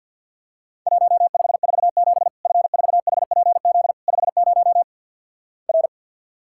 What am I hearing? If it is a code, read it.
Morse code, heard as a keyed tone, about 50 wpm, 700 Hz. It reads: 1547 F4HPZ 50 R
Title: Worked All Europe DX Contest - QTC Exchange - 50wpm